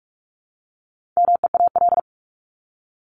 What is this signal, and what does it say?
Morse code, heard as a keyed tone, about 44 wpm, 700 Hz. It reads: MEAL
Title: Top 1751-2000 Words 45wpm